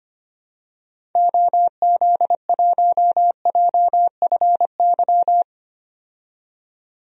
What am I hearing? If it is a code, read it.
Morse code, heard as a keyed tone, about 25 wpm, 700 Hz. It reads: OZ1JFY
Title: DXCC-100 Call Signs - 6 Characters - 25wpm